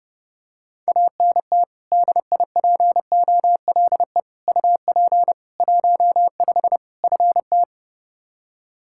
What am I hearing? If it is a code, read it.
Morse code, heard as a keyed tone, about 30 wpm, 700 Hz. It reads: ANT DIPOLE UP 15 FT
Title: Practice QSO - Exchange 2 - Info - 30wpm